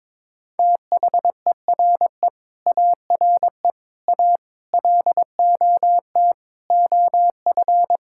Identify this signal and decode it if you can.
Morse code, heard as a keyed tone, about 22 wpm, 700 Hz. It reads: THERE ARE A LOT OF
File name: practice / 22wpm / Sets of 8 Words - 22wpm